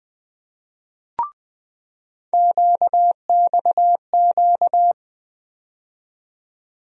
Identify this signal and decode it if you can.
Morse code, heard as a keyed tone, about 20 wpm, 700 Hz. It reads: QXQ
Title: USA Call Sign Postfixes - 3 Letters 20wpm